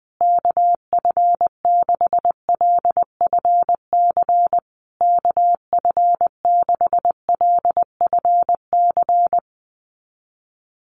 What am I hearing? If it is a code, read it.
Morse code, heard as a keyed tone, about 20 wpm, 700 Hz. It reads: KF6LFC KF6LFC
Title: USA Call Signs Repeated 3x 20wpm